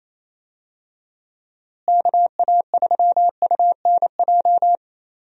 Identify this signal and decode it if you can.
Morse code, heard as a keyed tone, about 28 wpm, 700 Hz. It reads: KA3UNJ
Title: USA Call Signs - 6 Characters - Repeated 3x - 28wpm